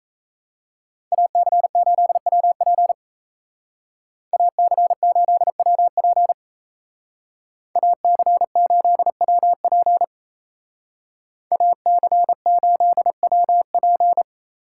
Morse code, heard as a keyed tone, about 36 wpm, 700 Hz. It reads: AC8WP AC8WP AC8WP AC8WP